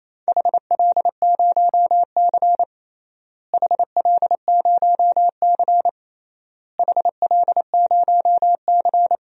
Morse code, heard as a keyed tone, about 27 wpm, 700 Hz. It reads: HL0C HL0C HL0C
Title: DXCC-100 Call Signs - 4 Characters - Repeated 3x - 28wpm